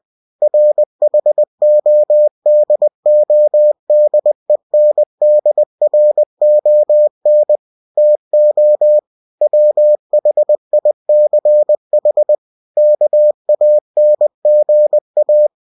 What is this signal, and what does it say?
Morse code, heard as a keyed tone, about 20 wpm, 600 Hz. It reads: RHODODENDRON TO WHICH KANGA
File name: cw-books / winnie-the-pooh / winnie-20wpm-page-0085